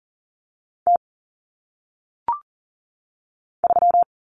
Morse code, heard as a keyed tone, about 38 wpm, 700 Hz. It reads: T 3
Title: Common Cut Numbers - 1-Digit 40wpm